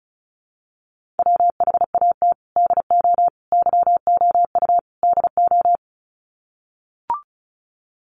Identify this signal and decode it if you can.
Morse code, heard as a keyed tone, about 34 wpm, 700 Hz. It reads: WHAT DO YOU DO
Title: Sets of 4 Words 35wpm